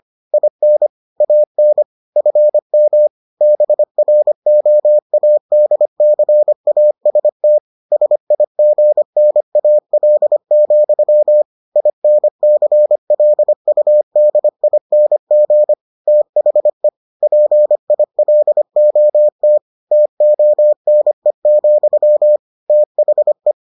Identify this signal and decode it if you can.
Morse code, heard as a keyed tone, about 25 wpm, 600 Hz. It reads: IN AN FM BROADCAST SIGNAL, INCLUDING THE PILOT TONE, THE